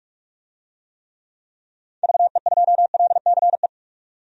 Morse code, heard as a keyed tone, about 47 wpm, 700 Hz. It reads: VE2LCE
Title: DXCC-100 Call Signs - 6 Characters - 45wpm